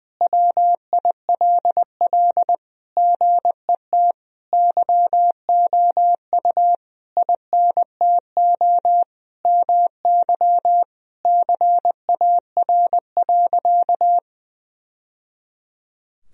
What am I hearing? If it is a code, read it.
Morse code, heard as a keyed tone, about 20 wpm, 700 Hz. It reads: WILL GET YOU INTO MY CAR.
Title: Sentences from Top 1000 Words - Review Part 1 - 20wpm